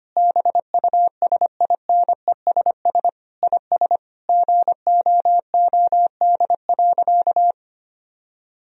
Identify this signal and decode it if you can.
Morse code, heard as a keyed tone, about 25 wpm, 700 Hz. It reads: BUSINESS IS GOOD.
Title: Sentences from Top 1000 Words - Review Part 4 - 25wpm